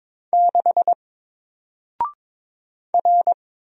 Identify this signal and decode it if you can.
Morse code, heard as a keyed tone, about 22 wpm, 700 Hz. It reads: 6 R
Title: Single Letter-Number Alt 22wpm